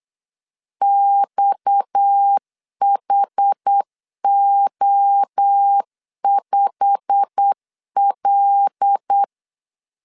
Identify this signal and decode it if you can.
Morse code, heard as a keyed tone, about 8 wpm, 800 Hz. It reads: XHO5L
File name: LEZIONI / 8 YJ45 / 8 YJ45 8wpm 3214